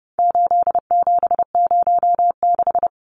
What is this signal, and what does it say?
Morse code, heard as a keyed tone, about 29 wpm, 700 Hz. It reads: 8706
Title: Numbers - 4-Digit 30wpm